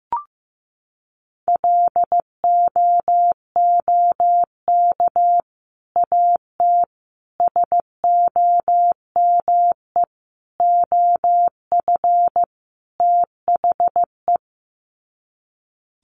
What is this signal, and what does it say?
Morse code, heard as a keyed tone, about 15 wpm, 700 Hz. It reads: LOOK AT SOME OF THE
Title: Sets of 5 Words Encore 15wpm